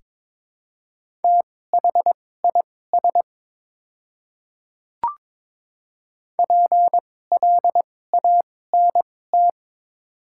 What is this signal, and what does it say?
Morse code, heard as a keyed tone, about 22 wpm, spaced as for 16 wpm, 690 Hz. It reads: THIS PLANT
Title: 2x Character Spacing - Top 500 Words - 22wpm